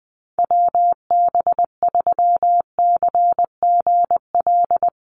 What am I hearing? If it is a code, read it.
Morse code, heard as a keyed tone, about 20 wpm, 700 Hz. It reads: WB3CGL
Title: USA Call Signs Repeated 2x 20wpm